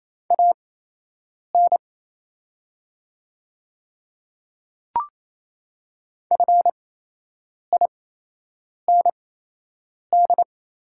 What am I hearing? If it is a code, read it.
Morse code, heard as a keyed tone, about 28 wpm, spaced as for 8 wpm, 700 Hz. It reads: AN FIND
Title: 8x Character Spacing - Top 100 Words - 28wpm